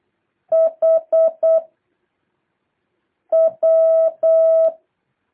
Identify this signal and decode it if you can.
Morse code, heard as a keyed tone, about 8 wpm, 650 Hz. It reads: H W
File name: jh8cbh / 2023cw